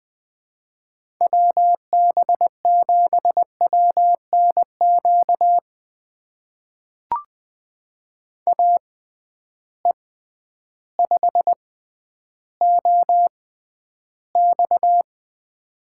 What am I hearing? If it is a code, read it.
Morse code, heard as a keyed tone, about 20 wpm, spaced as for 11 wpm, 700 Hz. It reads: WB7WNQ AE5OX
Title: USA Call Signs - 6x Character Spacing - 20wpm